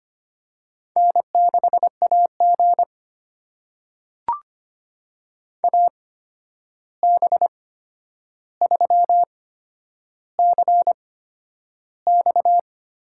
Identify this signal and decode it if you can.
Morse code, heard as a keyed tone, about 25 wpm, spaced as for 10 wpm, 700 Hz. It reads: N6AG AB3CX
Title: USA Call Signs - 8x Character Spacing - 25wpm